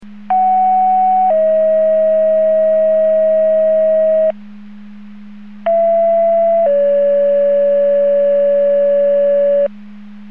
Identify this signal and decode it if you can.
Two-tone page, heard as a one-second tone, then a three-second tone, about 750 then 640 Hz, about 670 then 570 Hz. It is heard from 0.3 to 9.7 s.